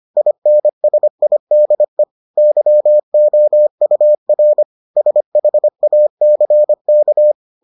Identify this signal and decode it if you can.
Morse code, heard as a keyed tone, about 25 wpm, 600 Hz. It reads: INSIDE YOUR SHACK